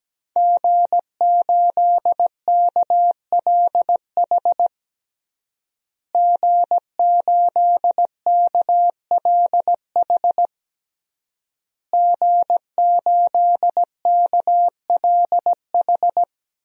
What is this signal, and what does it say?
Morse code, heard as a keyed tone, about 17 wpm, 700 Hz. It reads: G8KLH G8KLH G8KLH
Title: DXCC-100 Call Signs - 5 Characters - Repeated 3x - 17wpm